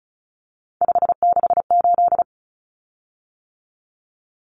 Morse code, heard as a keyed tone, about 34 wpm, 700 Hz. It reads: 568
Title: Common Cut Numbers - 3-Digit 35wpm